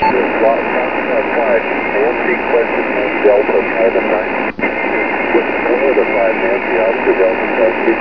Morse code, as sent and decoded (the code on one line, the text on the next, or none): none